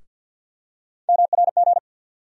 Morse code, heard as a keyed tone, about 48 wpm, 700 Hz. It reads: MRG
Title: USA Call Sign Postfixes - 3 Letters 50wpm